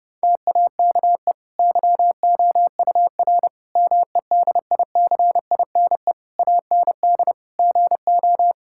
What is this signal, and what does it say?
Morse code, heard as a keyed tone, about 32 wpm, 700 Hz. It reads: TAKE YOUR MEDICINE AND GO
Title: Sentences from Top 1000 Words - Review Part 6 - 30wpm